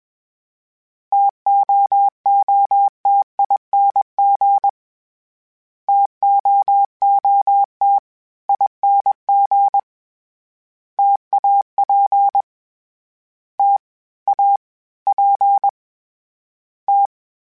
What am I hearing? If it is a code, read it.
Morse code, heard as a keyed tone, about 21 wpm, 800 Hz. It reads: TOOTING TOOT ING TAP T A P T
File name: Variable Speed / Variable Speed-mp3s / T AOEI's sp @20